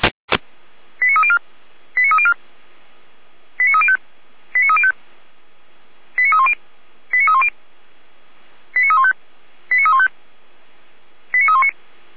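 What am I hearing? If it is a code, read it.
Five-tone selective call (ZVEI), heard as quick runs of five tones, tones of about 70 ms. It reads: 89304 89304 89305 89305 89310 89310 89315 89315 89319
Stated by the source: ZVEI five-tone selective call received on an SDR.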